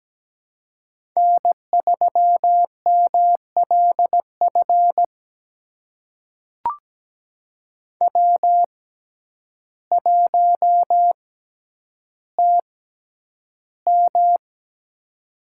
Morse code, heard as a keyed tone, about 17 wpm, spaced as for 9 wpm, 700 Hz. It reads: N3MLF W1TM
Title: USA Call Signs - 6x Character Spacing - 17wpm